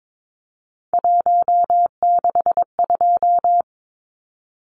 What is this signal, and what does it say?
Morse code, heard as a keyed tone, about 22 wpm, 700 Hz. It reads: 162
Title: Numbers - 3-Digit 22wpm